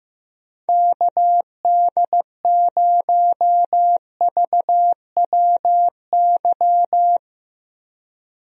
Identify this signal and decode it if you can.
Morse code, heard as a keyed tone, about 16 wpm, 700 Hz. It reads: KD0VWY
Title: USA Call Signs - 6 Characters - Repeated 2x - 15wpm